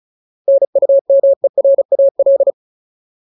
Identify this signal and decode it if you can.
Morse code, heard as a keyed tone, about 35 wpm, 550 Hz. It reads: NUMERAL
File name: Ultimate Word List / 35WPM Ultimate